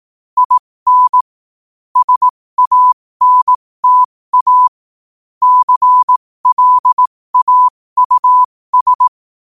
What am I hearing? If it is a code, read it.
Morse code, heard as a keyed tone, about 18 wpm, spaced as for 15 wpm, 1000 Hz. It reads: IN SANTA CLAUS
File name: qotdfiles / 15wpm